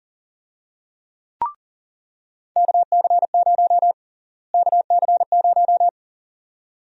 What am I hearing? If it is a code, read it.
Morse code, heard as a keyed tone, about 41 wpm, 700 Hz. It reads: KC0 KC0
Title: USA Call Sign Prefixes - Repeated 2x - 40wpm